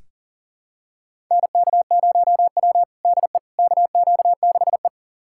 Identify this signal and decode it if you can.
Morse code, heard as a keyed tone, about 40 wpm, 700 Hz. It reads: NK0W DE KQ6E
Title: USA Callsign DE Callsign - 4 Characters - 40wpm